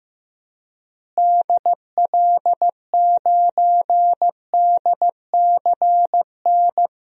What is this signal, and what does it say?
Morse code, heard as a keyed tone, about 15 wpm, 700 Hz. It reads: DL9DCN
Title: DXCC-100 Call Signs - 6 Characters - Repeated 3x - 15wpm